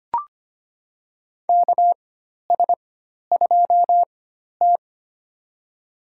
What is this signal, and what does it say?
Morse code, heard as a keyed tone, about 25 wpm, spaced as for 12 wpm, 700 Hz. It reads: KS2T
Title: USA Call Signs - 4 Characters - 4x Character Spacing - 25wpm